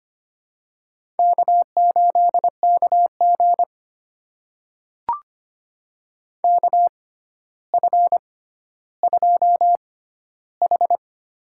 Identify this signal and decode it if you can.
Morse code, heard as a keyed tone, about 25 wpm, spaced as for 13 wpm, 700 Hz. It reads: K8KG KF2H
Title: USA Call Signs - 4 Characters - 6x Character Spacing - 25wpm